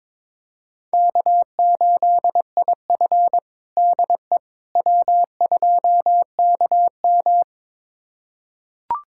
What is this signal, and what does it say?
Morse code, heard as a keyed tone, about 22 wpm, 700 Hz. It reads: K8IF DE W2KM
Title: USA Callsign DE Callsign - 4 Characters - 22wpm